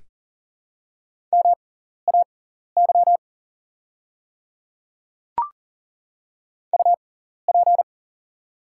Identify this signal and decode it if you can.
Morse code, heard as a keyed tone, about 42 wpm, spaced as for 14 wpm, 690 Hz. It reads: MAY UP
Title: 6x Character Spacing - Top 100 Words - 40wpm